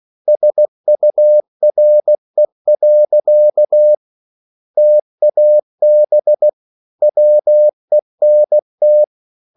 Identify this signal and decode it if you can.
Morse code, heard as a keyed tone, about 16 wpm, 600 Hz. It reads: SURE. TAB WENT